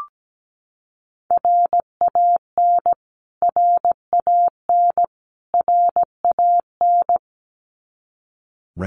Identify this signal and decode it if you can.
Morse code, heard as a keyed tone, about 17 wpm, 700 Hz. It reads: RAN RAN RAN
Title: Top 500 Words Repeated 17wpm